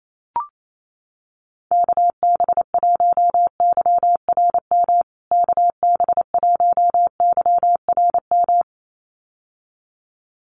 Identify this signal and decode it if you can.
Morse code, heard as a keyed tone, about 28 wpm, 700 Hz. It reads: KB1YRM KB1YRM
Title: USA Call Signs Repeated 2x 28wpm